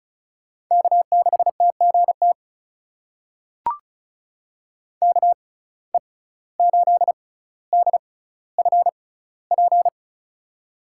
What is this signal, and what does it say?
Morse code, heard as a keyed tone, about 36 wpm, spaced as for 17 wpm, 700 Hz. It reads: K6TGT KE8DFP
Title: USA Call Signs - 6x Character Spacing - 35wpm